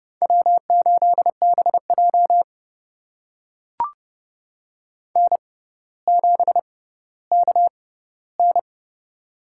Morse code, heard as a keyed tone, about 32 wpm, spaced as for 16 wpm, 700 Hz. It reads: W8BJ N7KN
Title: USA Call Signs - 4 Characters - 6x Character Spacing - 30wpm